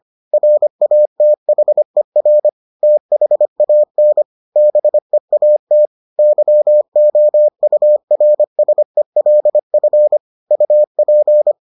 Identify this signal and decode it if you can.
Morse code, heard as a keyed tone, about 25 wpm, 600 Hz. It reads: RATHER THAN BEAT YOURSELF UP